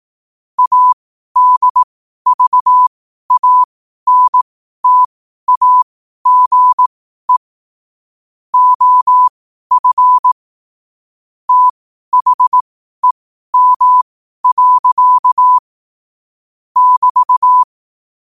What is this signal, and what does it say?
Morse code, heard as a keyed tone, about 18 wpm, spaced as for 12 wpm, 1000 Hz. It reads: ADVANTAGE OF THEM. <BT>
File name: qotdfiles / 12wpm